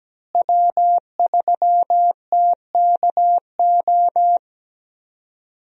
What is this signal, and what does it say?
Morse code, heard as a keyed tone, about 17 wpm, 700 Hz. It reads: W3TKO
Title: USA Call Signs - 5 Characters - Repeated 3x - 17wpm